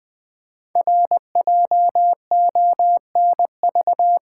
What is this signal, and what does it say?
Morse code, heard as a keyed tone, about 20 wpm, 700 Hz. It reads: RJONV
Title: Code Groups - 5 Letters - 20wpm